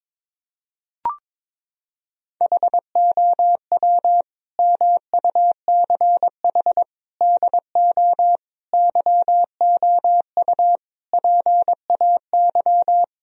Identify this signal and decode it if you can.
Morse code, heard as a keyed tone, about 22 wpm, 700 Hz. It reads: HOW MUCH DO YOU PAY
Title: Sentences from Top 1000 Words - Review Part 3 - 22wpm